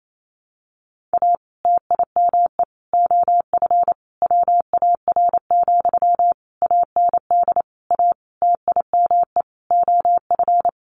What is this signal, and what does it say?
Morse code, heard as a keyed tone, about 28 wpm, 700 Hz. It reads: A TIME OF WAR, AND A TIME OF
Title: Sentences from Top 1500 Words 28wpm